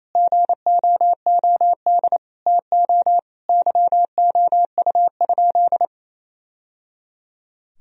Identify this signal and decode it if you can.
Morse code, heard as a keyed tone, about 28 wpm, 700 Hz. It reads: GOOD TO YOU?
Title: Sentences from Top 1000 Words - Review Part 4 - 28wpm